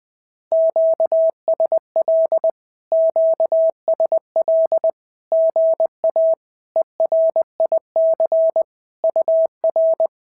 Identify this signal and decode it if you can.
Morse code, heard as a keyed tone, about 20 wpm, 650 Hz. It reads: QSL QSL GA ERIC UR
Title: SOTA QSO Practice 20wpm